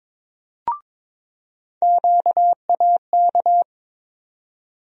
Morse code, heard as a keyed tone, about 22 wpm, 700 Hz. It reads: QAK
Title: USA Call Sign Postfixes - 3 Letters 22wpm